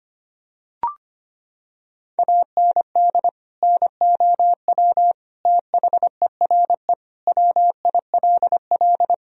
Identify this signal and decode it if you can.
Morse code, heard as a keyed tone, about 25 wpm, 700 Hz. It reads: AND NOW THERE WILL
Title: Sentences from Top 1000 Words - Review Part 2 - 25wpm